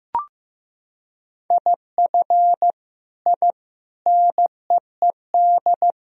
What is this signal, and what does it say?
Morse code, heard as a keyed tone, about 15 wpm, 700 Hz. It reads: IF I NEED
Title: Sentences from Top 1000 Words - Review Part 1 - 15wpm